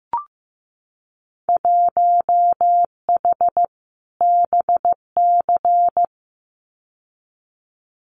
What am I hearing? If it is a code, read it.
Morse code, heard as a keyed tone, about 15 wpm, 700 Hz. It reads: 1H BC
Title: Winter Field Day Exchanges Encore 15wpm